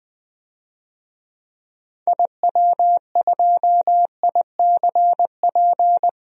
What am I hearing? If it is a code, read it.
Morse code, heard as a keyed tone, about 20 wpm, 700 Hz. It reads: IW2ICP
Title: DXCC-100 Call Signs - 6 Characters - Repeated 2x - 20wpm